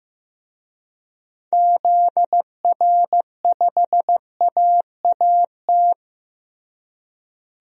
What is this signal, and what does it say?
Morse code, heard as a keyed tone, about 15 wpm, 700 Hz. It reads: ZR5AAT
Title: DXCC-100 Call Signs - 6 Characters - Repeated 2x - 15wpm